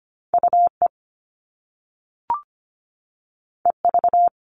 Morse code, heard as a keyed tone, about 25 wpm, 700 Hz. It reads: UE EV
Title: All Cut Numbers - 2-Digit 25wpm